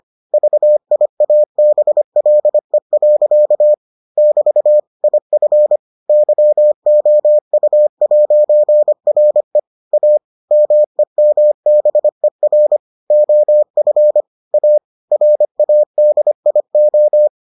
Morse code, heard as a keyed tone, about 25 wpm, 600 Hz. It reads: VIABLE. <BT> IF YOU'RE A MEMBER OF A RADIO